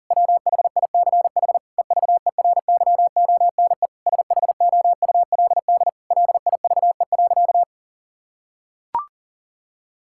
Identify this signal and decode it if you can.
Morse code, heard as a keyed tone, about 40 wpm, 700 Hz. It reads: WHICH EVERYONE SHOULD LIVE.
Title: Sentences from Top 1000 Words - Review Part 5 - 40wpm